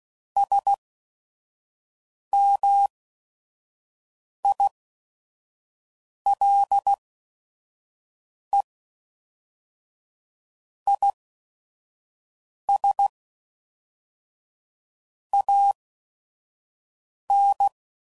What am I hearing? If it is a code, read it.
Morse code, heard as a keyed tone, about 16 wpm, spaced as for 5 wpm, 800 Hz. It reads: SMILE IS AN